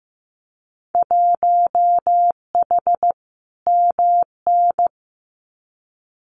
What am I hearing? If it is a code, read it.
Morse code, heard as a keyed tone, about 15 wpm, 700 Hz. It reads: 1H MN
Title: Winter Field Day Exchanges Encore 15wpm